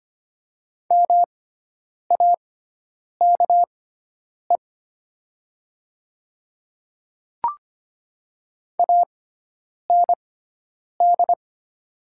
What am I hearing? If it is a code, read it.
Morse code, heard as a keyed tone, about 25 wpm, spaced as for 9 wpm, 700 Hz. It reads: MAKE AND
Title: 6x Character Spacing - Top 100 Words - 25wpm